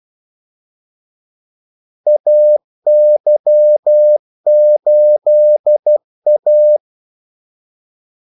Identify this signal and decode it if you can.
Morse code, heard as a keyed tone, about 12 wpm, 600 Hz. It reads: AY8A
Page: 2 ekstra ordmellemrum mellem kaldesignalerne